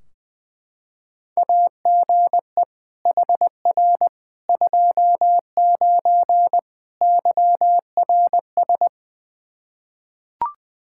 Morse code, heard as a keyed tone, about 21 wpm, 700 Hz. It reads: AGE HR 29 YRS
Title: Practice QSO - Exchange 3 - More Info - 20wpm